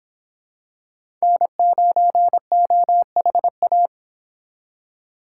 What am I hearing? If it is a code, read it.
Morse code, heard as a keyed tone, about 26 wpm, 700 Hz. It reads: N9OHA